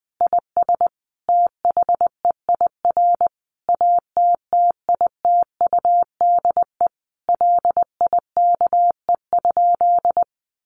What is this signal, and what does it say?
Morse code, heard as a keyed tone, about 20 wpm, 700 Hz. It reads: IS THEIR ATTITUDE LIKE?
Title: Sentences from Top 1300 Words 20wpm